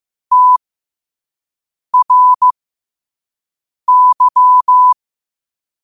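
Morse code, heard as a keyed tone, about 15 wpm, spaced as for 5 wpm, 1000 Hz. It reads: TRY
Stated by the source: Morse quote-of-the-day practice at 5 wpm.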